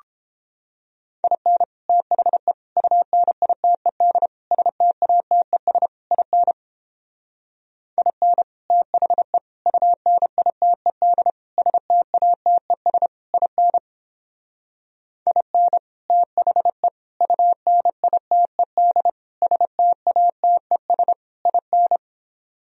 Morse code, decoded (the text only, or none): IN THE UNITED STATES IN IN THE UNITED STATES IN IN THE UNITED STATES IN